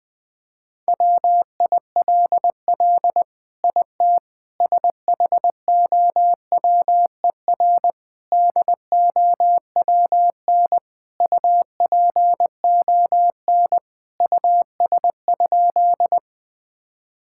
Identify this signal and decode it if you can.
Morse code, heard as a keyed tone, about 20 wpm, 700 Hz. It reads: WILL IT SHOWER DOWN UPON US?
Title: Sentences from Top 1600 Words 20wpm